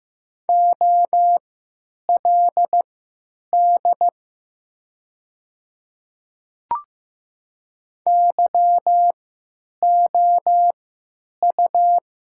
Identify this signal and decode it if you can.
Morse code, heard as a keyed tone, about 15 wpm, spaced as for 9 wpm, 700 Hz. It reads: OLD YOU
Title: 3x Character Spacing - Top 500 Words - 15wpm